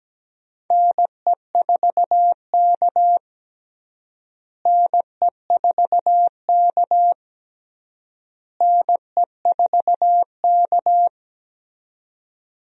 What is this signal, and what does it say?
Morse code, heard as a keyed tone, about 17 wpm, 700 Hz. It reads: NE4K NE4K NE4K
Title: USA Call Signs - 4 Characters - Repeated 3x - 17wpm